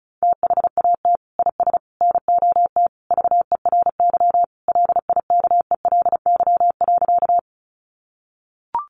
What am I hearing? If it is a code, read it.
Morse code, heard as a keyed tone, about 34 wpm, 700 Hz. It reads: THAT IS NOT VERY LIKELY.
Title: Sentences from Top 600 Words 35wpm